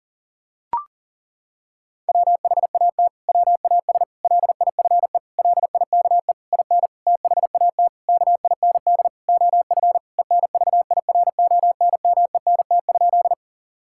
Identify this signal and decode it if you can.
Morse code, heard as a keyed tone, about 40 wpm, 700 Hz. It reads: WHAT WAS LIFE LIKE IN THAT KIND OF ENVIRONMENT?
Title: Sentences from Top 1000 Words - Review Part 5 - 40wpm